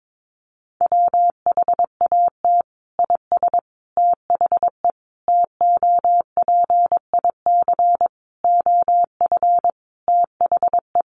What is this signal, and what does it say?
Morse code, heard as a keyed tone, about 22 wpm, 700 Hz. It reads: WHAT IS THE TOPIC OF THE
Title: Sentences from Top 1200 Words 22wpm